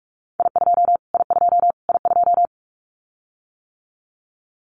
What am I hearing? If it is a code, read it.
Morse code, heard as a keyed tone, about 44 wpm, 700 Hz. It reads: I2 I2 I2
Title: Callsign Prefixes for Top 25 DXCC - Repeated 3x - 45wpm